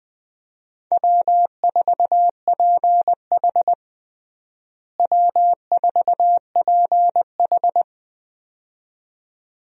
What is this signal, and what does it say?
Morse code, heard as a keyed tone, about 20 wpm, 700 Hz. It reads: W4PH W4PH
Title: USA Call Signs - 4 Characters - Repeated 2x - 20wpm